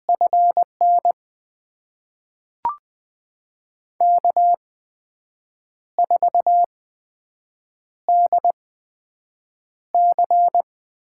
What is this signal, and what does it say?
Morse code, heard as a keyed tone, about 20 wpm, spaced as for 7 wpm, 700 Hz. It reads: FN K4DC
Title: USA Call Signs - 8x Character Spacing - 20wpm